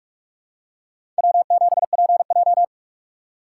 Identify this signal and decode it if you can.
Morse code, heard as a keyed tone, about 47 wpm, 700 Hz. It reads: W7PJ
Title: USA Call Signs - 4 Characters - 6x Character Spacing - 45wpm